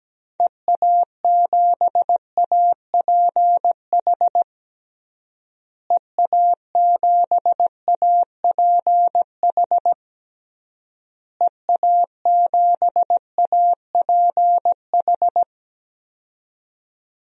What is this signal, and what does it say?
Morse code, heard as a keyed tone, about 17 wpm, 700 Hz. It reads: EA7APH EA7APH EA7APH
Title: DXCC-100 Call Signs - 6 Characters - Repeated 3x - 17wpm